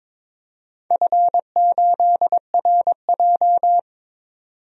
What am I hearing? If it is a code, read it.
Morse code, heard as a keyed tone, about 22 wpm, 700 Hz. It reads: F8RJ
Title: DXCC-100 Call Signs - 4 Characters - 22wpm